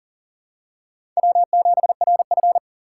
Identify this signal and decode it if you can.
Morse code, heard as a keyed tone, about 40 wpm, 700 Hz. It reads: W7RF
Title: USA Call Signs - 3x Character Spacing - 40wpm